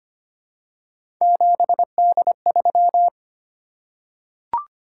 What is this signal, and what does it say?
Morse code, heard as a keyed tone, about 25 wpm, 700 Hz. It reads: 7D3
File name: Three Letter-Number Combinations - 25wpm